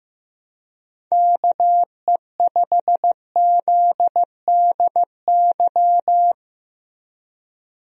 Morse code, heard as a keyed tone, about 15 wpm, 700 Hz. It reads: KE5ZDY
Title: USA Call Signs - 6 Characters - Repeated 3x - 15wpm